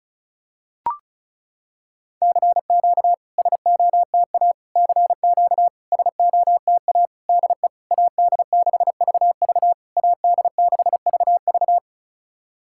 Morse code, heard as a keyed tone, about 34 wpm, 690 Hz. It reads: CQ SOTA CQ SOTA DE AD6VV AD6VV
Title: SOTA QSO Practice 35wpm